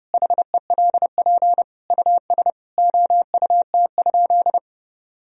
Morse code, heard as a keyed tone, about 30 wpm, 700 Hz. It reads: HELP US OUT?
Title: Sentences from Top 1000 Words - Review Part 3 - 30wpm